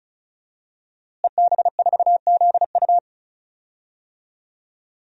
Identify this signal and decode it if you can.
Morse code, heard as a keyed tone, about 34 wpm, 700 Hz. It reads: EB4ZU
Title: DXCC-100 Call Signs - 5 Characters - Repeated 2x - 35wpm